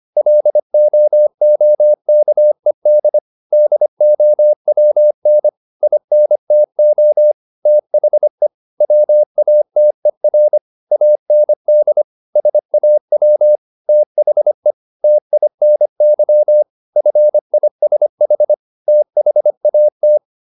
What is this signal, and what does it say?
Morse code, heard as a keyed tone, about 25 wpm, 600 Hz. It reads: LOOKED DOWN INTO THE WATER AND SAW THE TINY FISH THAT
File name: the old man and the sea part 1 25wpm